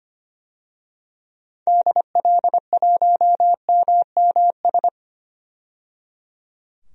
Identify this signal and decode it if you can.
Morse code, heard as a keyed tone, about 25 wpm, 700 Hz. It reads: DL1MMS
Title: DXCC-100 Call Signs - 6 Characters - 25wpm